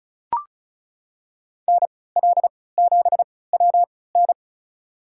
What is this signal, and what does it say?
Morse code, heard as a keyed tone, about 36 wpm, spaced as for 20 wpm, 700 Hz. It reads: NL7WN
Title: USA Call Signs - 3x Character Spacing - 35wpm